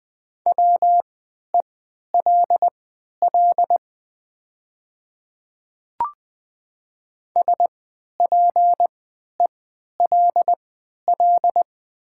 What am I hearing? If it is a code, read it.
Morse code, heard as a keyed tone, about 20 wpm, spaced as for 11 wpm, 700 Hz. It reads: WELL SPELL
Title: 3x Character Spacing - Top 200 Words - 20wpm